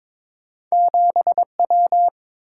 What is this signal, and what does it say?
Morse code, heard as a keyed tone, about 30 wpm, 700 Hz. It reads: 7 W